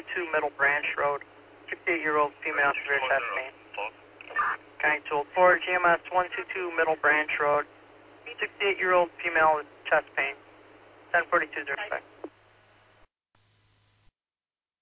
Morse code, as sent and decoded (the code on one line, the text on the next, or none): none